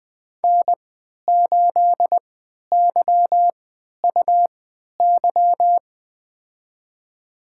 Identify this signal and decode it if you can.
Morse code, heard as a keyed tone, about 20 wpm, spaced as for 11 wpm, 700 Hz. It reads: N8YUY
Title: USA Call Signs - 5 Characters - 3x Character Spacing - 20wpm